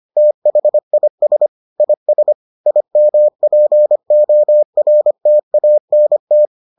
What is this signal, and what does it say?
Morse code, heard as a keyed tone, about 25 wpm, 600 Hz. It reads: THIS IS IMPORTANT